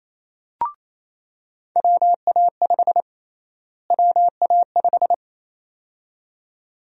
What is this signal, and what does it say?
Morse code, heard as a keyed tone, about 28 wpm, 700 Hz. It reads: WA5 WA5
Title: USA Call Sign Prefixes - Repeated 2x - 28wpm